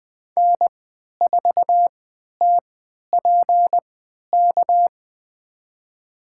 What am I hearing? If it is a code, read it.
Morse code, heard as a keyed tone, about 20 wpm, spaced as for 11 wpm, 700 Hz. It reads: N4TPK
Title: USA Call Signs - 3x Character Spacing - 20wpm